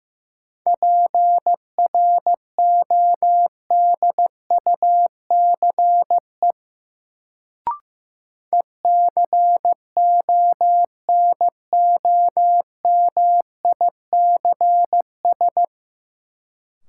Morse code, heard as a keyed tone, about 16 wpm, 700 Hz. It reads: PRODUCE ECONOMICS
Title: Sentences from Top 1000 Words - Review Part 3 - New Word Drill - 15wpm